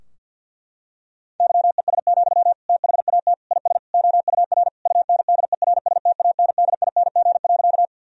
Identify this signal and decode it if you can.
Morse code, heard as a keyed tone, about 50 wpm, 700 Hz. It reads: YES, THAT IS OUR UNDERSTANDING.
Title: Sentences from Top 1000 Words - Review Part 3 - 50wpm